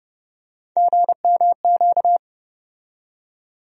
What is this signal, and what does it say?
Morse code, heard as a keyed tone, about 30 wpm, 700 Hz. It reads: GMQ
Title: USA Call Sign Postfixes - 3 Letters 30wpm